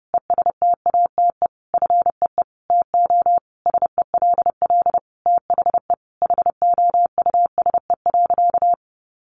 Morse code, heard as a keyed tone, about 30 wpm, 700 Hz. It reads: ESTATE FEE TO SELL THE HOUSE.
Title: Sentences from Top 1500 Words 30wpm